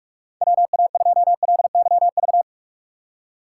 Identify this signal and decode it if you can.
Morse code, heard as a keyed tone, about 46 wpm, 700 Hz. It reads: WA2LYV
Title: USA Call Signs - 6 Characters - 4x Character Spacing - 45wpm